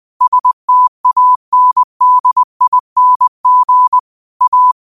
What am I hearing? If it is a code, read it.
Morse code, heard as a keyed tone, about 20 wpm, 1000 Hz. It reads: STANDING A
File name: qotdfiles / 20wpm